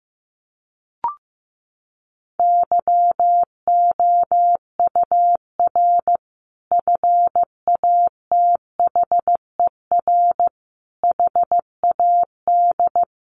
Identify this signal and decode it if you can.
Morse code, heard as a keyed tone, about 15 wpm, 700 Hz. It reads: YOUR FATHER HAD
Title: Sentences from Top 1100 Words 15wpm